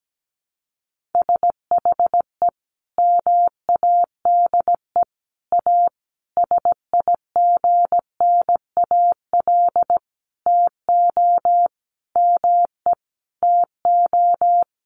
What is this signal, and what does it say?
Morse code, heard as a keyed tone, about 17 wpm, 700 Hz. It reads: SHE MADE A SIGNAL TO ME TO
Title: Sentences from Top 1400 Words 17wpm